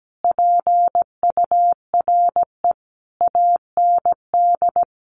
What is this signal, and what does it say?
Morse code, heard as a keyed tone, about 17 wpm, 700 Hz. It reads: PURE AND
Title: Sentences from Top 1500 Words 17wpm